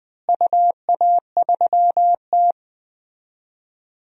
Morse code, heard as a keyed tone, about 19 wpm, 700 Hz. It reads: UA3T
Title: DXCC-100 Call Signs - 4 Characters - Repeated 3x - 20wpm